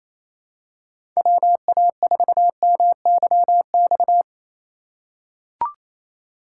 Morse code, heard as a keyed tone, about 28 wpm, 700 Hz. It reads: WA4MYX
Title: USA Call Signs - 6 Characters - 3x Character Spacing - 28wpm